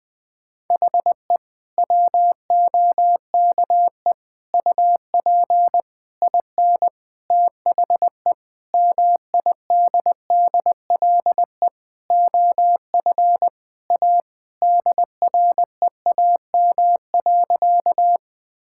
Morse code, heard as a keyed tone, about 20 wpm, 700 Hz. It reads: HE WOKE UP IN THE MIDDLE OF A DREAM.
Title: Sentences with 4-Word Phrases - 20wpm